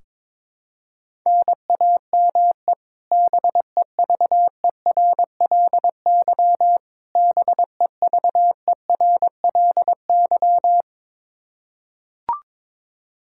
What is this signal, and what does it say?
Morse code, heard as a keyed tone, about 21 wpm, 700 Hz. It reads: NAME BEVERLY BEVERLY
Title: Practice QSO - Exchange 1 - Beginning - 22wpm